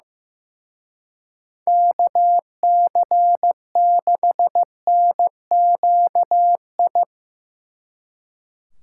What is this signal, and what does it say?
Morse code, heard as a keyed tone, about 16 wpm, 700 Hz. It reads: KC6NQI
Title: Speed Racing - USA 2x3 Call Signs 15wpm